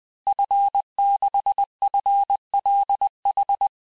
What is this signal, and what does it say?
Morse code, heard as a keyed tone, about 20 wpm, 800 Hz. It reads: F6FLH
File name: cours-du-mercredi / audios / cours-858-Fin-de-QSO-vitesse-19mpm